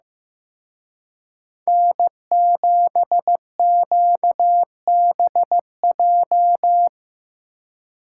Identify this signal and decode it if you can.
Morse code, heard as a keyed tone, about 18 wpm, 700 Hz. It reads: N7QBJ
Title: Speed Racing - USA Call Signs 15wpm